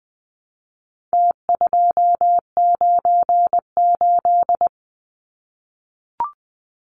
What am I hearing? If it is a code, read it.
Morse code, heard as a keyed tone, about 20 wpm, 700 Hz. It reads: T298
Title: Numbers - Contest Style - 4-Digit 20wpm